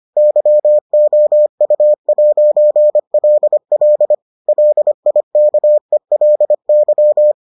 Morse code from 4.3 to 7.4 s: LIKELY

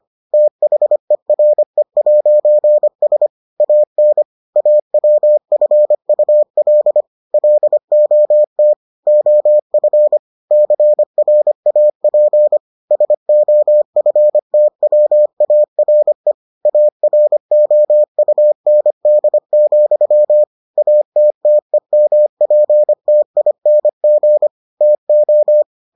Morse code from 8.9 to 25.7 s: OF CRAP SOFTWARE AROUND, ATTEMPTING TO

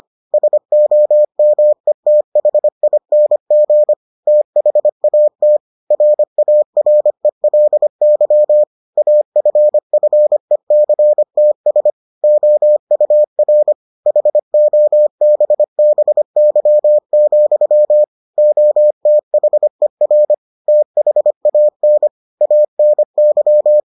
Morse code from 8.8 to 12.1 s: AFFECTS

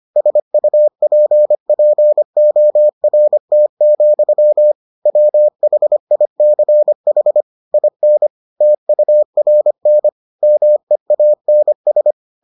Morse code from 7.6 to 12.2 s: IN TURN MEANS

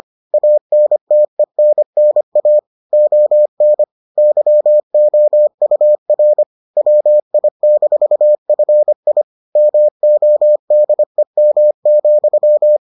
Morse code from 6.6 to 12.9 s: WI-FI MODEM,

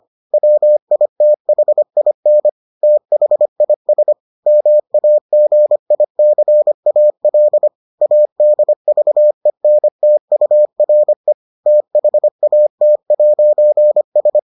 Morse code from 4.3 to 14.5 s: MAGICAL ADVENTURE THAT'S

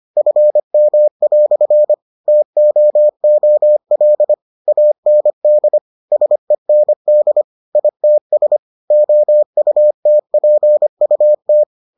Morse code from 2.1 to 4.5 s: TOOL

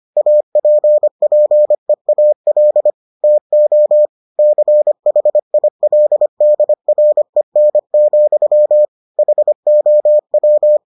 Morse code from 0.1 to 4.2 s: APPEAL TO

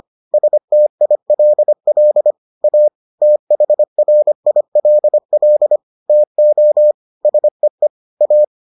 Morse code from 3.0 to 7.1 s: THRILL TO

